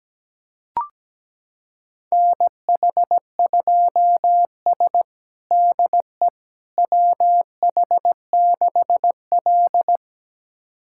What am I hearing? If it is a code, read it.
Morse code, heard as a keyed tone, about 17 wpm, 700 Hz. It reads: NH2S DE WH6L
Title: USA Callsign DE Callsign - 4 Characters - 17wpm